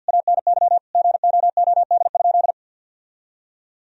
Morse code, heard as a keyed tone, about 51 wpm, 700 Hz. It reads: ANY GOOD?
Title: Sentences from Top 1000 Words - Review Part 4 - 50wpm